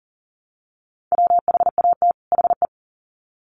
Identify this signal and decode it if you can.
Morse code, heard as a keyed tone, about 40 wpm, 700 Hz. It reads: WHAT HE
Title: Sets of 2 Words 40wpm